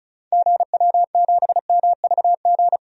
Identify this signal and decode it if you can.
Morse code, heard as a keyed tone, about 36 wpm, 690 Hz. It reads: GW7MVG
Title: DXCC-100 Call Signs - 6 Characters - 35wpm